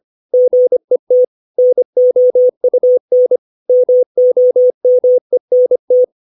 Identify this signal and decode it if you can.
Morse code, heard as a keyed tone, about 25 wpm, 500 Hz. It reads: GET NOUN MOMENT
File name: 632 words / 632 Words 25WPM